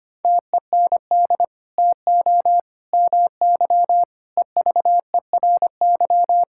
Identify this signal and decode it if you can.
Morse code, heard as a keyed tone, about 25 wpm, 700 Hz. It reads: TEND TO MY EVERY
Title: Sentences from Top 1000 Words - Review Part 5 - 25wpm